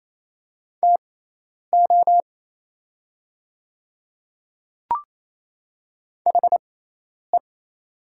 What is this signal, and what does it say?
Morse code, heard as a keyed tone, about 27 wpm, spaced as for 10 wpm, 700 Hz. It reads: TO HE
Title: 6x Character Spacing - Top 100 Words - 28wpm